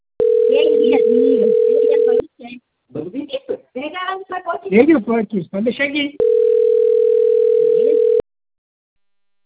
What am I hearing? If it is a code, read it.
Ringback tone (North America), 440 + 480 Hz, with speech.
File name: rg-1600-unknown- - - .93931